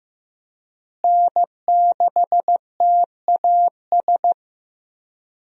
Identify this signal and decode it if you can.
Morse code, heard as a keyed tone, about 15 wpm, 700 Hz. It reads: N6TAS
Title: USA Call Signs - 5 Characters - Repeated 2x - 15wpm